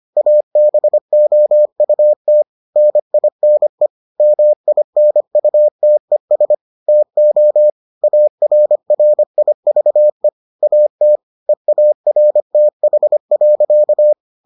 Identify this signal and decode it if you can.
Morse code, heard as a keyed tone, about 25 wpm, 600 Hz. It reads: ABOUT NINE MINUTES TO ARRIVE AT EARTH.